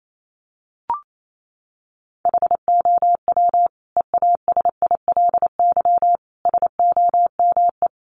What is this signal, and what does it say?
Morse code, heard as a keyed tone, about 28 wpm, 700 Hz. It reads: HOW EASILY SOME
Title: Sentences from Top 900 Words 28wpm